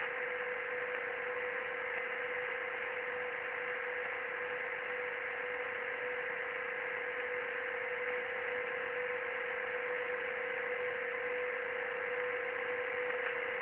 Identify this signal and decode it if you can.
PSK31, about 31.25 baud.
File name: ARCHIV / BPSK CW / OK1IF-OK1ANG- -1230CET-3766.5-3767